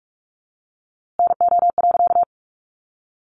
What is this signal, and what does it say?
Morse code, heard as a keyed tone, about 44 wpm, 700 Hz. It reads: NO.
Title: A Sound Of Thunder 45wpm